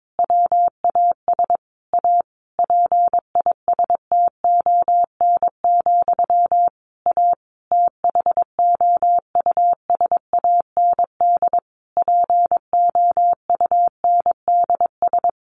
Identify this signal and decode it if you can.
Morse code, heard as a keyed tone, about 22 wpm, 700 Hz. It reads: WAS A PISTON, A THOUSAND POUNDS
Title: A Sound Of Thunder 22wpm